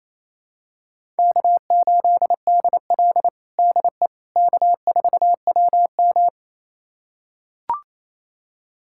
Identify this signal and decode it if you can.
Morse code, heard as a keyed tone, about 28 wpm, 700 Hz. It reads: K8DL DE K4WM
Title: USA Callsign DE Callsign - 4 Characters - 28wpm